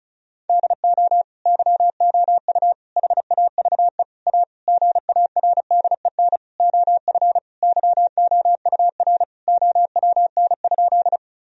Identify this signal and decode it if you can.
Morse code, heard as a keyed tone, about 36 wpm, 700 Hz. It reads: DO YOU HAVE A GARDEN OF YOUR OWN?
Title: Sentences from Top 1000 Words - Review Part 4 - 35wpm